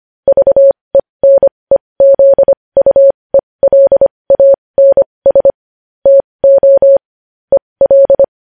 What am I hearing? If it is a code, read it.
Morse code, heard as a keyed tone, about 25 wpm, spaced as for 20 wpm, 550 Hz. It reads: VENEZUELANS TO EL
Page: News Headlines in Morse Code at 20 WPM